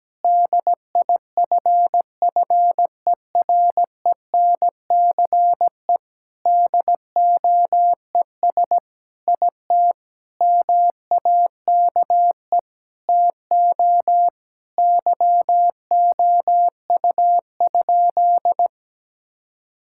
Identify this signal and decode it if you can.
Morse code, heard as a keyed tone, about 17 wpm, 700 Hz. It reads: DIFFERENCE DOES IT MAKE TO YOU?
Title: Sentences from Top 1000 Words - Review Part 5 - 17wpm